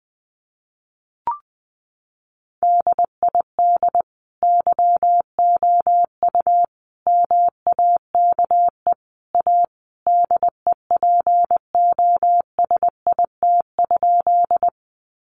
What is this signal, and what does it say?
Morse code, heard as a keyed tone, about 20 wpm, 700 Hz. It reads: DID YOU MAKE A DEPOSIT?
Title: Sentences from Top 1300 Words 20wpm